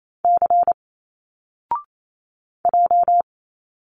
Morse code, heard as a keyed tone, about 28 wpm, 700 Hz. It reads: C J
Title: Single Letters 28wpm